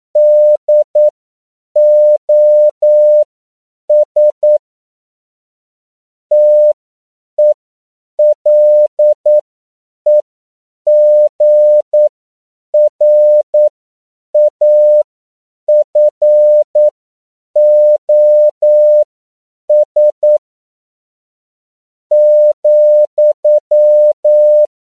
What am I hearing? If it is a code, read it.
Morse code, heard as a keyed tone, about 9 wpm, spaced as for 7 wpm, 600 Hz. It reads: DOS TELEGRAFOS,